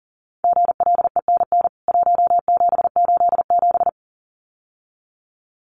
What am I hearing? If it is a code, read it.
Morse code, heard as a keyed tone, about 40 wpm, 700 Hz. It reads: GLENN 1787
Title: CWT Exchange 40wpm